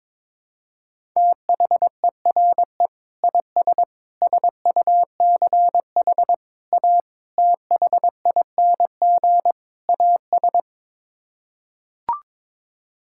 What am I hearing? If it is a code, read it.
Morse code, heard as a keyed tone, about 22 wpm, 700 Hz. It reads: THERE IS SUCH A THING AS
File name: practice / 22wpm / Sets of 6 Words - 22wpm